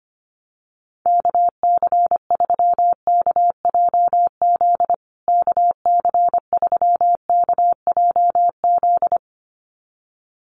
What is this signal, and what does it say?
Morse code, heard as a keyed tone, about 25 wpm, 700 Hz. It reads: KC3KJZ KC3KJZ
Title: USA Call Signs Repeated 2x 25wpm